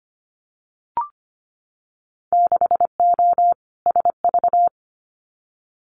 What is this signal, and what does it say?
Morse code, heard as a keyed tone, about 25 wpm, 700 Hz. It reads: 6O SV
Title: Winter Field Day Exchanges Encore 25wpm